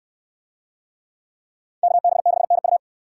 Morse code, heard as a keyed tone, about 69 wpm, 700 Hz. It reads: KB6NU